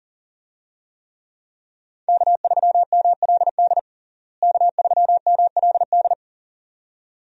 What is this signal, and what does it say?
Morse code, heard as a keyed tone, about 40 wpm, 700 Hz. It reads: K3MLD K3MLD
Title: USA Call Signs - 5 Characters - Repeated 2x - 40wpm